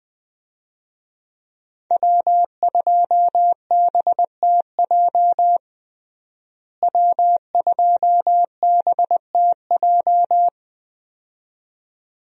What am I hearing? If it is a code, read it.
Morse code, heard as a keyed tone, about 21 wpm, 700 Hz. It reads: W2BTJ W2BTJ
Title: USA Call Signs - 5 Characters - Repeated 2x - 20wpm